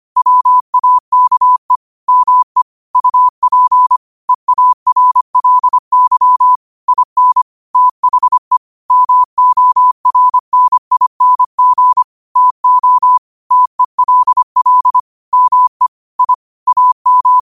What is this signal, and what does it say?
Morse code, heard as a keyed tone, about 25 wpm, 1000 Hz. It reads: WAKE ME UP EARLY IN THE MORNING TO TELL ME I AM
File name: qotdfiles / 25wpm